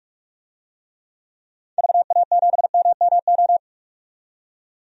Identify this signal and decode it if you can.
Morse code, heard as a keyed tone, about 45 wpm, 700 Hz. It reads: VA7MMO
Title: DXCC-100 Call Signs - 6 Characters - 45wpm